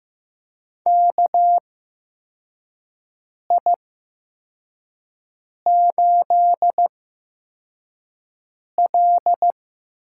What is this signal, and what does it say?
Morse code, heard as a keyed tone, about 15 wpm, spaced as for 4 wpm, 700 Hz. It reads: KI8L
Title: USA Call Signs - 4 Characters - 8x Character Spacing - 15wpm